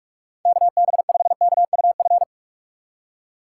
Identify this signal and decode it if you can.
Morse code, heard as a keyed tone, about 44 wpm, 700 Hz. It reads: KB5KUF